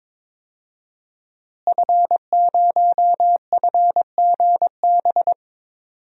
Morse code, heard as a keyed tone, about 22 wpm, 700 Hz. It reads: F0FGB
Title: DXCC-100 Call Signs - 5 Characters - Repeated 3x - 22wpm